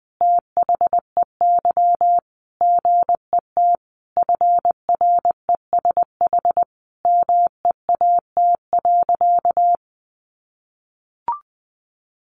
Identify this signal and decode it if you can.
Morse code, heard as a keyed tone, about 20 wpm, 700 Hz. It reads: THEY GET FRESH MEAT.
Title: Sentences from Top 1200 Words 20wpm